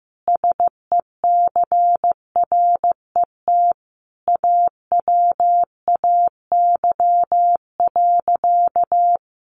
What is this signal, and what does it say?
Morse code, heard as a keyed tone, about 15 wpm, 700 Hz. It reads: SECRET AWAY.
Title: Sentences from Top 1200 Words 15wpm